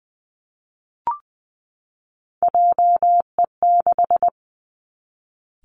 Morse code, heard as a keyed tone, about 20 wpm, 700 Hz. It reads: JE6
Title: Three Challenging Letter-Number Combinations 20wpm